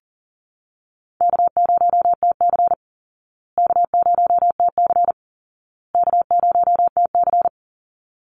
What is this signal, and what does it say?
Morse code, heard as a keyed tone, about 40 wpm, 700 Hz. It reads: K0TC K0TC K0TC
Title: Easy Call Signs Repeated 3x - 40wpm